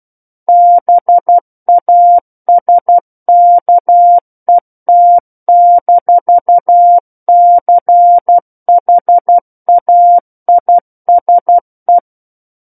Morse code, heard as a keyed tone, about 12 wpm, 700 Hz. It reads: BASKET-CHAISE
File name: Book One / 03-Chapter 3 - 12 WPM